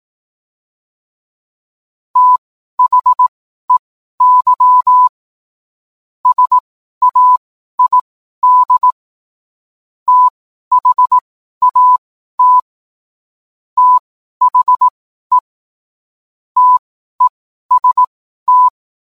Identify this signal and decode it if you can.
Morse code, heard as a keyed tone, about 18 wpm, spaced as for 12 wpm, 1000 Hz. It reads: THEY SAID THAT THE TEST